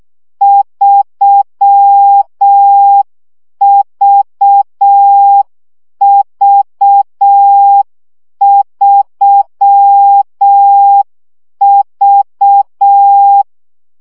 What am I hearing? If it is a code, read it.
Morse code, heard as a keyed tone, about 6 wpm, 800 Hz. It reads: 3VV3V
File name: CW gyakorlas v 3 5 baud